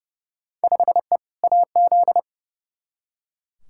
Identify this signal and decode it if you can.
Morse code, heard as a keyed tone, about 30 wpm, 700 Hz. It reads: 5E AZ
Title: Field Day Exchanges - 30wpm